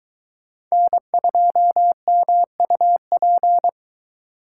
Morse code, heard as a keyed tone, about 22 wpm, 700 Hz. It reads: N2MUP